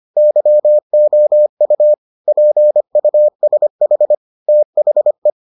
Morse code, heard as a keyed tone, about 25 wpm, 600 Hz. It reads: YOU PUSH THE